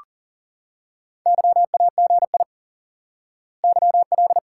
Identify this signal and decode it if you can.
Morse code, heard as a keyed tone, about 40 wpm, 700 Hz. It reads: YAGI YL
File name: IQE Course 3 - Lesson 89 - YAGI - Two Elements - 40wpm